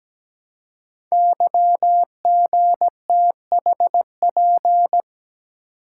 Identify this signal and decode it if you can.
Morse code, heard as a keyed tone, about 17 wpm, 700 Hz. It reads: YGTHP
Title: Code Groups - 5 Letters - 17wpm